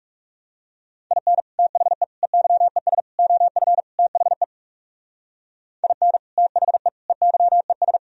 Morse code, heard as a keyed tone, about 38 wpm, 700 Hz. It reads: IN THE EYES OF THE IN THE EYES
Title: Speed-Racing - Sets of 5 Words 30wpm